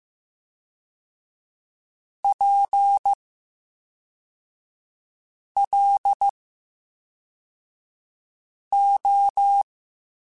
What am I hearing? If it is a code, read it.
Morse code, heard as a keyed tone, about 15 wpm, spaced as for 3 wpm, 800 Hz. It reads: PLO
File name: Special INGs / Special INGs-mp3s / VerySlowToFaster Ing @10